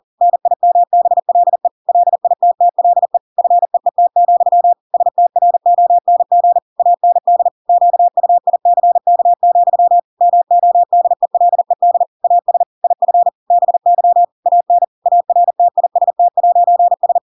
Morse code, heard as a keyed tone, about 40 wpm, 700 Hz. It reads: NIMBLE LITTLE FEET, STRONG AND QUICK, MODELED AS IF BY AN ARTIST'S
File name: Pinocchio / Pinocchio ch8 40 40 ik1hge